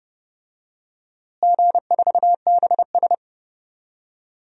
Morse code, heard as a keyed tone, about 30 wpm, 700 Hz. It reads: G4BS
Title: DXCC-100 Call Signs - 4 Characters - Repeated 3x - 30wpm